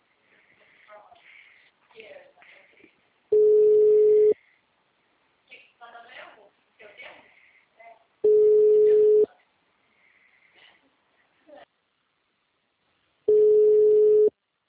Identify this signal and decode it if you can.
Ringback tone (Europe, 425 Hz), with speech.
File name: out- -2033- - - .20